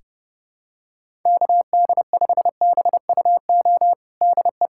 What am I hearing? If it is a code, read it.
Morse code, heard as a keyed tone, about 30 wpm, 700 Hz. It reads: KD5BUO DE
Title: USA Callsign DE Callsign - 6 Characters - 30wpm